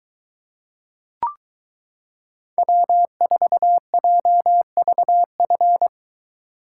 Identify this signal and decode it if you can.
Morse code, heard as a keyed tone, about 22 wpm, 700 Hz. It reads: W4JVF